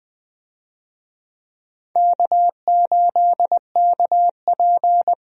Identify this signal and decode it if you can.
Morse code, heard as a keyed tone, about 20 wpm, 700 Hz. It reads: K8KP
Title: USA Call Signs - 4 Characters - Repeated 3x - 20wpm